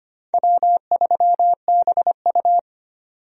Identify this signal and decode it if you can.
Morse code, heard as a keyed tone, about 25 wpm, 700 Hz. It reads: W3BU
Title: USA Call Signs - 4 Characters - 25wpm